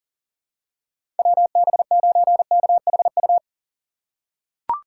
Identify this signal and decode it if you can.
Morse code, heard as a keyed tone, about 40 wpm, 700 Hz. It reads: WB9KHU
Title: USA Call Signs - 6 Characters - 6x Character Spacing - 40wpm